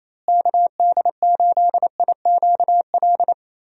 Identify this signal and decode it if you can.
Morse code, heard as a keyed tone, about 27 wpm, 700 Hz. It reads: KD8IQL
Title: USA Call Signs - 6 Characters - Repeated 3x - 28wpm